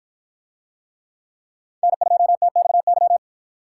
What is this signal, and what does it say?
Morse code, heard as a keyed tone, about 53 wpm, 700 Hz. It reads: N2TXY